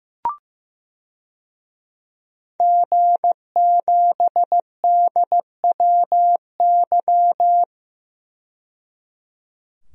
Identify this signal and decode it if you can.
Morse code, heard as a keyed tone, about 15 wpm, 700 Hz. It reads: G7DWY
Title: DXCC-100 Call Signs - 5 Characters - 15wpm